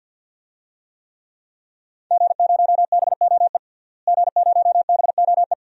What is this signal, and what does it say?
Morse code, heard as a keyed tone, about 46 wpm, 700 Hz. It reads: G0BOE G0BOE
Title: DXCC-100 Call Signs - 5 Characters - Repeated 2x - 50wpm